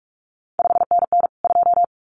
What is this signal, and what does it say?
Morse code, heard as a keyed tone, about 46 wpm, 700 Hz. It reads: 5NN 2
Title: CQ WW DX Contest Exchange Encore - 45wpm